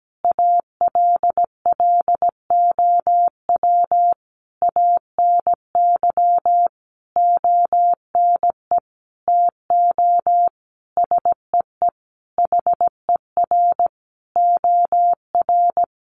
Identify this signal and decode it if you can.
Morse code, heard as a keyed tone, about 17 wpm, 700 Hz. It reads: ALLOW ANY ONE TO SEE HER OR
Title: Sentences from Top 1200 Words 17wpm